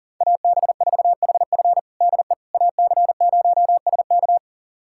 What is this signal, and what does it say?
Morse code, heard as a keyed tone, about 40 wpm, 700 Hz. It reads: AB4HF DE AC0SK
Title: USA Callsign DE Callsign - 5 Characters - 40wpm